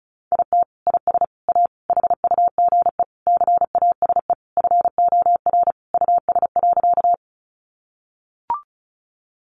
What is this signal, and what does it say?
Morse code, heard as a keyed tone, about 34 wpm, 700 Hz. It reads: IT IS A HUGE CASE FOR US.
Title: Sentences from Top 1100 Words 35wpm